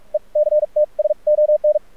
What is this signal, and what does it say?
Morse code, heard as a keyed tone, about 49 wpm, spaced as for 37 wpm, 600 Hz, with radio noise. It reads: ECTS ON
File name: archived / CRSnet